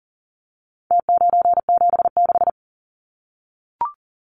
Morse code, heard as a keyed tone, about 40 wpm, 700 Hz. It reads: T976
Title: Numbers - Contest Style - 4-Digit 40wpm